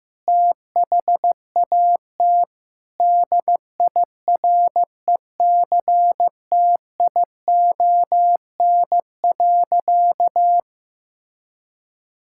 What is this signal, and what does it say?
Morse code, heard as a keyed tone, about 16 wpm, 700 Hz. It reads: THAT DIRECTION.
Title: Sentences from Top 1000 Words - Review Part 5 - 15wpm